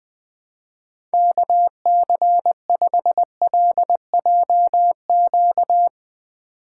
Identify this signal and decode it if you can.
Morse code, heard as a keyed tone, about 20 wpm, 700 Hz. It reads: KC5LJQ
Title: Speed Racing - USA 2x3 Call Signs 17wpm